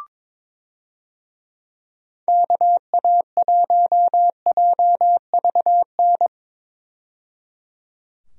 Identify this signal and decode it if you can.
Morse code, heard as a keyed tone, about 22 wpm, 700 Hz. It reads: KA1JVN
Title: USA Call Signs - 6 Characters - 22wpm